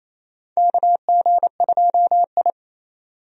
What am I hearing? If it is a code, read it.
Morse code, heard as a keyed tone, about 27 wpm, 700 Hz. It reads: KG2I
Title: USA Call Signs - 4 Characters - 28wpm